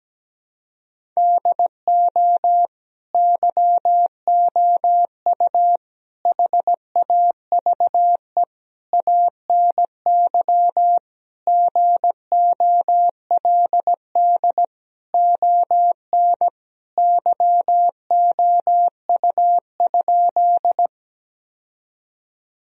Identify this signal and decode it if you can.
Morse code, heard as a keyed tone, about 17 wpm, 700 Hz. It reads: DO YOU HAVE ANY GOLD ON YOU?
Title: Sentences from Top 1000 Words - Review Part 3 - 17wpm